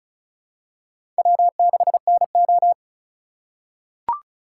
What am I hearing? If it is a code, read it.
Morse code, heard as a keyed tone, about 36 wpm, 690 Hz. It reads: W6NO
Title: USA Call Signs - 4 Characters - 3x Character Spacing - 35wpm